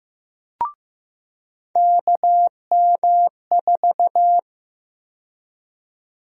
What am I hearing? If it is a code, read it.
Morse code, heard as a keyed tone, about 15 wpm, 700 Hz. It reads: KM4
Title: USA Call Sign Prefixes - 15wpm